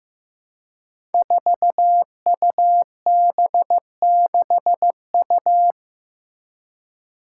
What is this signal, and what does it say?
Morse code, heard as a keyed tone, about 15 wpm, 700 Hz. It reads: 4UB6U
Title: Five Challenging Letter-Number Combinations 15wpm